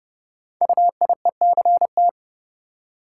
Morse code, heard as a keyed tone, about 30 wpm, 700 Hz. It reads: UIECT
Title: Code Groups - 5 Letters - 30wpm